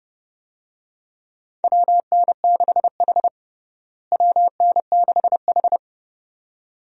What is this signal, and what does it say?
Morse code, heard as a keyed tone, about 30 wpm, 700 Hz. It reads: WN6H WN6H
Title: USA Call Signs - 4 Characters - Repeated 2x - 30wpm